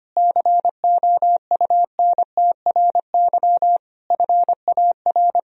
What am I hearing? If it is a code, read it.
Morse code, heard as a keyed tone, about 25 wpm, 700 Hz. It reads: COUNTRY FAR
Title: Sentences from Top 1000 Words - Review Part 3 - 25wpm